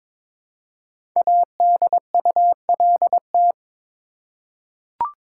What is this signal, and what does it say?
Morse code, heard as a keyed tone, about 22 wpm, 700 Hz. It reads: ADULT
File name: 5 Letter Words - First 3 Letters Given - 22wpm